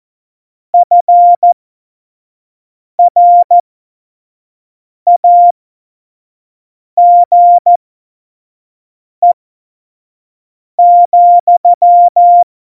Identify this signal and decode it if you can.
Morse code, heard as a keyed tone, about 14 wpm, spaced as for 5 wpm, 700 Hz. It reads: FRAGE,
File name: Morsekurs / Witze / Witz 05. September 17, Tempo 25